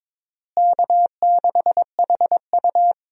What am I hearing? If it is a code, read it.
Morse code, heard as a keyed tone, about 22 wpm, 700 Hz. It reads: K6HU
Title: USA Call Signs - 4 Characters - Repeated 3x - 22wpm